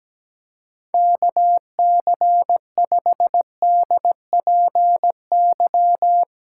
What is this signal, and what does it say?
Morse code, heard as a keyed tone, about 17 wpm, 700 Hz. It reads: KC5DPY
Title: Speed Racing - USA 2x3 Call Signs 17wpm